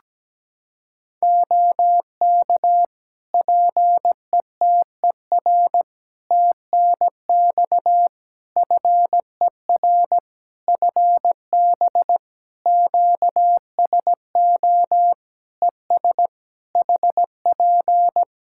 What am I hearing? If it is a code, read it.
Morse code, heard as a keyed tone, about 17 wpm, 700 Hz. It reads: OK PETER TNX FER FB QSO ES HP
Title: Practice QSO - Exchange 4 - Ending - 17wpm